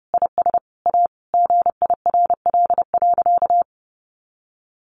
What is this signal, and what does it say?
Morse code, heard as a keyed tone, about 30 wpm, 700 Hz. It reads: IS A GIRL.
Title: Additional Easy Phrases 30wpm